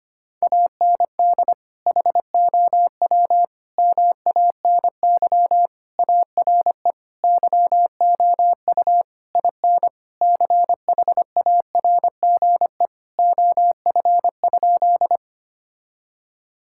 Morse code, heard as a keyed tone, about 25 wpm, 700 Hz. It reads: AND HOW MANY ARE YOU IN CHARGE OF?
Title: Sentences from Top 1000 Words - Review Part 6 - 25wpm